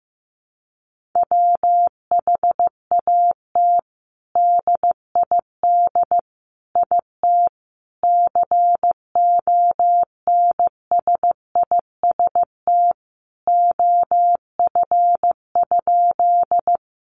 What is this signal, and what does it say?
Morse code, heard as a keyed tone, about 15 wpm, 700 Hz. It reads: WHAT DID IT CONSIST OF?
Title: Sentences from Top 1400 Words 15wpm